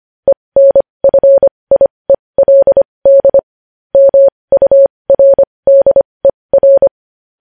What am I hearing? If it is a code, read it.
Morse code, heard as a keyed tone, about 25 wpm, spaced as for 20 wpm, 550 Hz. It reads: ENFIELD MURDER
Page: news-headlines-in-morse-code-at-20-wpm